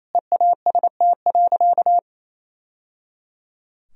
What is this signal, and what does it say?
Morse code, heard as a keyed tone, about 28 wpm, 700 Hz. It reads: EAST.
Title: Sentences from Top 1000 Words - Review Part 3 - 28wpm